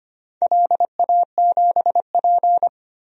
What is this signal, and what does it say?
Morse code, heard as a keyed tone, about 25 wpm, 700 Hz. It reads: LA7P
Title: DXCC-100 Call Signs - 4 Characters - Repeated 2x - 25wpm